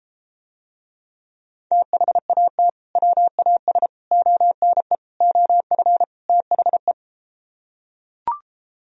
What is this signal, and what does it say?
Morse code, heard as a keyed tone, about 33 wpm, 700 Hz. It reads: THAT WAS ONE OF THE